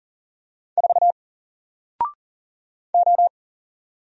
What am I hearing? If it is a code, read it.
Morse code, heard as a keyed tone, about 40 wpm, 700 Hz. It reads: 4 O
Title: Single Letter-Number Alt 40wpm